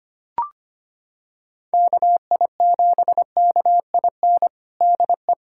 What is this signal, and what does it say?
Morse code, heard as a keyed tone, about 25 wpm, 700 Hz. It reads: KI7KIN DE
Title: USA Callsign DE Callsign 25wpm